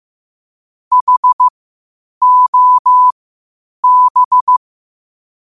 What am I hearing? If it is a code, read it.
Morse code, heard as a keyed tone, about 15 wpm, spaced as for 8 wpm, 1000 Hz. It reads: HOB
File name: qotdfiles / 8wpm